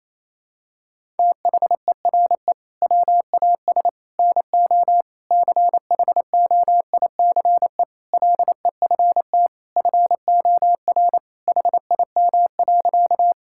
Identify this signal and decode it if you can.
Morse code, heard as a keyed tone, about 27 wpm, 700 Hz. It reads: THERE WAS NO CHOICE LEFT FOR HIM.
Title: Sentences from Top 1000 Words - Review Part 6 - 28wpm